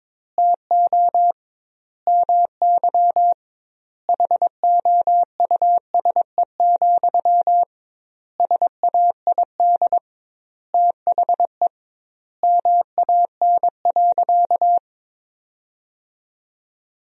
Morse code, heard as a keyed tone, about 22 wpm, 700 Hz. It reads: TO MY HOUSE, SAID THE MAN.
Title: 2x Word-Spacing - Sentences from Top 400 Words - 22wpm